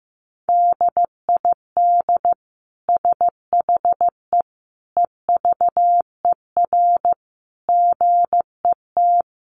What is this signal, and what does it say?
Morse code, heard as a keyed tone, about 15 wpm, 700 Hz. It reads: DID SHE EVER GET
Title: Sentences from Top 1100 Words 15wpm